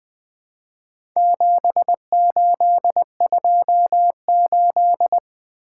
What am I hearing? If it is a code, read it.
Morse code, heard as a keyed tone, about 20 wpm, 700 Hz. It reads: 7828
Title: Four Letter-Number Combinations - 20wpm